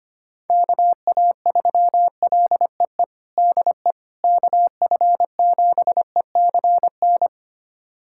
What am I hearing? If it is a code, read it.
Morse code, heard as a keyed tone, about 25 wpm, 700 Hz. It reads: KA3LEE DE KF7ECN
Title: USA Callsign DE Callsign - 6 Characters - 25wpm